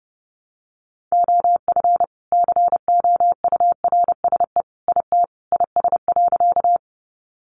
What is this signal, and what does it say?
Morse code, heard as a keyed tone, about 30 wpm, 700 Hz. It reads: OF COURSE IT IS.
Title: Sentences from Top 500 Words 30wpm